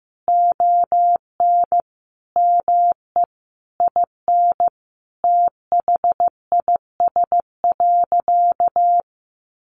Morse code, heard as a keyed tone, about 15 wpm, 700 Hz. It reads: ON ME IN THIS.
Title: Sentences from Top 1200 Words 15wpm